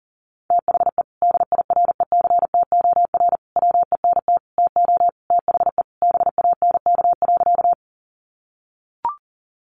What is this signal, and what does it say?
Morse code, heard as a keyed tone, about 40 wpm, 700 Hz. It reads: THE DIRECTOR WENT TO THE BANK.
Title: Sentences from Top 1300 Words 40wpm